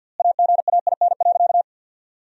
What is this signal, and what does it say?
Morse code, heard as a keyed tone, about 51 wpm, 700 Hz. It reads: AGAIN.
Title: 4x Word-Spacing - Sentences from Top 200 Words - 50wpm